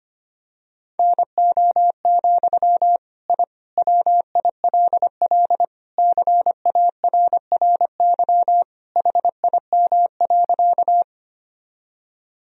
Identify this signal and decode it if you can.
Morse code, heard as a keyed tone, about 25 wpm, 700 Hz. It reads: NO, I WILL CARRY HIM.
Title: Sentences from Top 1000 Words - Review Part 1 - 25wpm